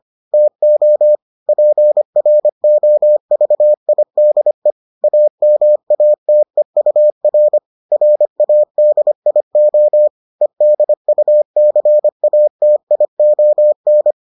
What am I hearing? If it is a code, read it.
Morse code, heard as a keyed tone, about 25 wpm, 600 Hz. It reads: TO PROVIDE AMATEUR RADIO EDUCATION